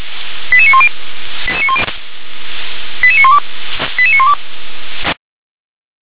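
Five-tone selective call (ZVEI), heard as quick runs of five tones, tones of about 70 ms. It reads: 88910 88912 88912
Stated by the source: ZVEI five-tone selective call 88841, received via SDR.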